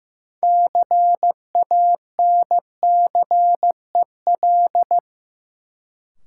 Morse code, heard as a keyed tone, about 16 wpm, 700 Hz. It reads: CANCEL
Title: Sentences from Top 1200 Words - New Word Drill - 15wpm